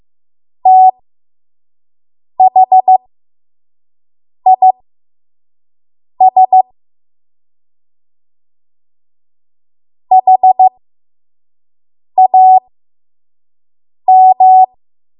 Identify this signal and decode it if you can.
Morse code, heard as a keyed tone, about 15 wpm, spaced as for 5 wpm, 750 Hz. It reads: THIS HAM
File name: Morse / 05WPM